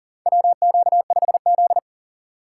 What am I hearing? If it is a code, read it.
Morse code, heard as a keyed tone, about 40 wpm, 700 Hz. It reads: WQ5Z
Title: USA Call Signs - 4 Characters - 2x Character Spacing - 40wpm